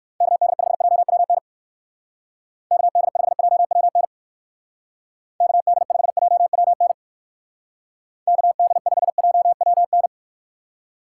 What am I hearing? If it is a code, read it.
Morse code, heard as a keyed tone, about 56 wpm, 700 Hz. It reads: KD5JWN KD5JWN KD5JWN KD5JWN
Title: Speed Racing - USA 2x3 Call Signs 45wpm